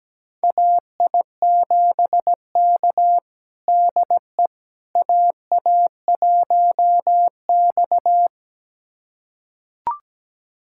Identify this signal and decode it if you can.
Morse code, heard as a keyed tone, about 17 wpm, 700 Hz. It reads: AI7K DE AA1X
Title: USA Callsign DE Callsign - 4 Characters - 17wpm